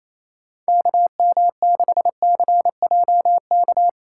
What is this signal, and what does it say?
Morse code, heard as a keyed tone, about 28 wpm, 700 Hz. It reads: KM6CJK
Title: USA Callsign DE Callsign - 6 Characters - 28wpm